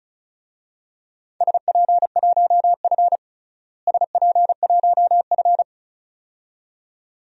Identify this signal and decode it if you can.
Morse code, heard as a keyed tone, about 36 wpm, 700 Hz. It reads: SP1F SP1F
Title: DXCC-100 Call Signs - 4 Characters - Repeated 2x - 35wpm